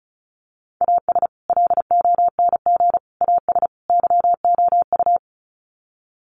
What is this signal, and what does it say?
Morse code, heard as a keyed tone, about 34 wpm, 700 Hz. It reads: AS LONG AS YOU
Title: Sets of 4 Words 35wpm